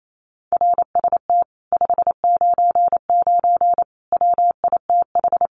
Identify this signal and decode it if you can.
Morse code, heard as a keyed tone, about 28 wpm, 700 Hz. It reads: RST 599 WITH
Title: Common QSO Phrases Encore 28wpm